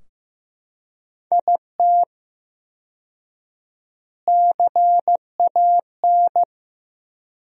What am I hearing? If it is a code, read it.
Morse code, heard as a keyed tone, about 15 wpm, 700 Hz. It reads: IT CAN
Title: 4x Word-Spacing - Sentences from Top 400 Words - 15wpm